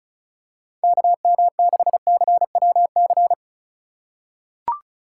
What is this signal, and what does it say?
Morse code, heard as a keyed tone, about 36 wpm, 700 Hz. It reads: KM6CWC
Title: USA Call Signs - 6 Characters - 3x Character Spacing - 35wpm